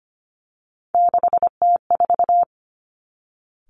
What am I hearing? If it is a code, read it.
Morse code, heard as a keyed tone, about 25 wpm, 700 Hz. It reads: 6T4
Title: Three Challenging Letter-Number Combinations 25wpm